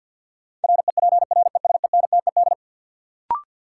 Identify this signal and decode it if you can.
Morse code, heard as a keyed tone, about 52 wpm, 700 Hz. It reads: REPRESENTED
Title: Sets of 5 Words - New Word Drill - 50wpm